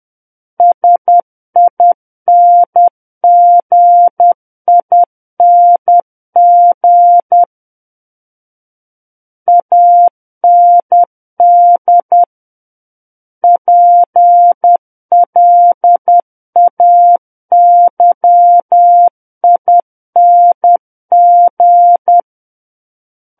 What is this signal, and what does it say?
Morse code, heard as a keyed tone, about 10 wpm, 700 Hz. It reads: SINGING AND PLAYING
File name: Book One / 01-Chapter 1 - 10 WPM